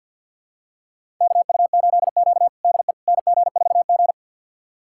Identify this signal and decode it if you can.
Morse code, heard as a keyed tone, about 52 wpm, 700 Hz. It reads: KU8Q DE NG4G
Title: USA Callsign DE Callsign - 4 Characters - 50wpm